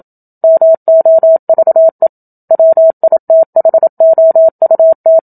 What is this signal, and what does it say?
Morse code, heard as a keyed tone, about 27 wpm, 650 Hz. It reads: MOVE WITHOUT
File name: Book Two / 01-Chapter 1 - 27 WPM